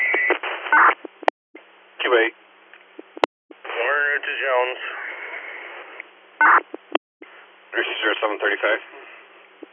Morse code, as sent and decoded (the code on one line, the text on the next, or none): none